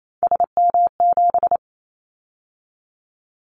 Morse code, heard as a keyed tone, about 28 wpm, 700 Hz. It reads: SM7
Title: Callsign Prefixes for Top 25 DXCC - 28wpm